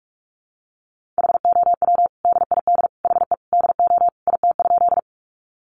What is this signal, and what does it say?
Morse code, heard as a keyed tone, about 44 wpm, 700 Hz. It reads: HOW DID HE DO IT?
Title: Sentences from Top 100 Words 45wpm